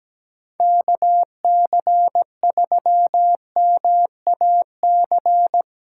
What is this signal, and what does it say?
Morse code, heard as a keyed tone, about 17 wpm, 700 Hz. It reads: KC3MAC
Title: USA Call Signs - 6 Characters - 3x Character Spacing - 17wpm